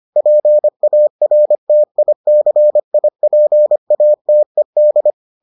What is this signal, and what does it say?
Morse code, heard as a keyed tone, about 25 wpm, 600 Hz. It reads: PARTICIPATED